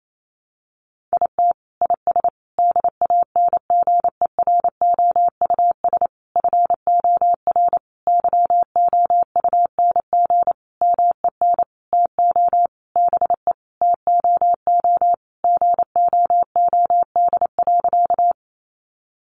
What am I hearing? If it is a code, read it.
Morse code, heard as a keyed tone, about 28 wpm, 700 Hz. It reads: IT IS DANGEROUS FOR YOUNG MEN TO BE TOO GOOD.
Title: Sentences from Top 1200 Words 28wpm